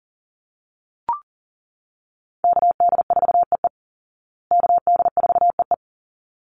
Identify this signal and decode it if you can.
Morse code, heard as a keyed tone, about 40 wpm, 700 Hz. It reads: KD4EE KD4EE
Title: Call Signs Repeated 3x - 40wpm